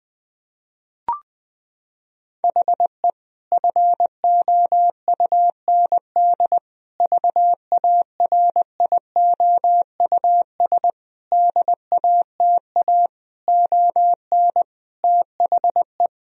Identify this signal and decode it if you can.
Morse code, heard as a keyed tone, about 21 wpm, 700 Hz. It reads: HE FOUND VARIOUS DATA ON THE
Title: Sentences from Top 1000 Words - Review Part 4 - 20wpm